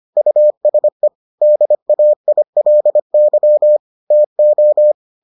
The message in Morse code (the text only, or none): USE DAILY TO